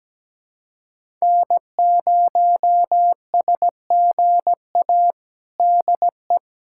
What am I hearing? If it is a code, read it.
Morse code, heard as a keyed tone, about 17 wpm, 700 Hz. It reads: N0SGA DE
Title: USA Callsign DE Callsign - 5 Characters - 17wpm